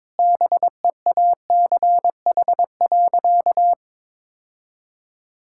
Morse code, heard as a keyed tone, about 22 wpm, 700 Hz. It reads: BEACH.
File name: practice / 22wpm / Sentences with 4-Word Phrases - 22wpm